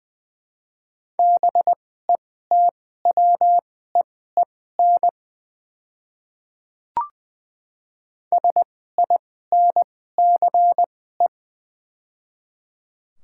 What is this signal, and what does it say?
Morse code, heard as a keyed tone, about 20 wpm, spaced as for 14 wpm, 700 Hz. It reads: BETWEEN SINCE
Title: 2x Character Spacing - Top 500 Words - 20wpm